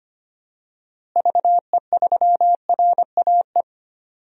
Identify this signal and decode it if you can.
Morse code, heard as a keyed tone, about 25 wpm, 700 Hz. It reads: VE3RAE
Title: DXCC-100 Call Signs - 6 Characters - Repeated 2x - 25wpm